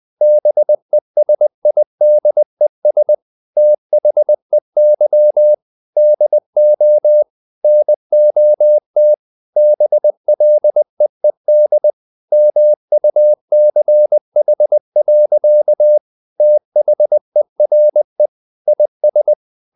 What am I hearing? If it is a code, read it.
Morse code, heard as a keyed tone, about 20 wpm, 600 Hz. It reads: BESIDES THEY DO NOT BLEED MUCH. THERE IS
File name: the old man and the sea part 2 20wpm